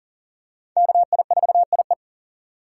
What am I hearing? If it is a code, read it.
Morse code, heard as a keyed tone, about 40 wpm, 700 Hz. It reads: KI4IE
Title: USA Call Signs - 5 Characters - 2x Character Spacing - 40wpm